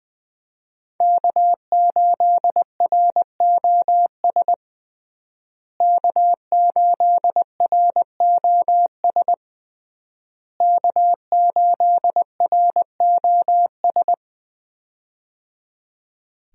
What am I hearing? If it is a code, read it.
Morse code, heard as a keyed tone, about 20 wpm, 700 Hz. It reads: K8ROS K8ROS K8ROS
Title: USA Call Signs - 5 Characters - Repeated 3x - 20wpm